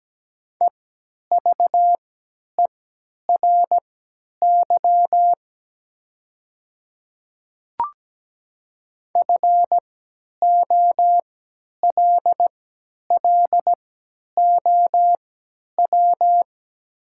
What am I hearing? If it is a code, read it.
Morse code, heard as a keyed tone, about 17 wpm, spaced as for 10 wpm, 700 Hz. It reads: EVERY FOLLOW
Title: 3x Character Spacing - Top 200 Words - 17wpm